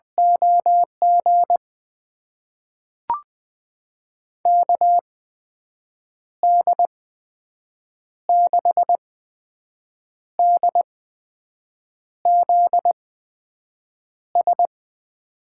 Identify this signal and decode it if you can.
Morse code, heard as a keyed tone, about 20 wpm, spaced as for 6 wpm, 700 Hz. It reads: OG KD6DZS
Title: USA Call Signs - 6 Characters - 8x Character Spacing - 20wpm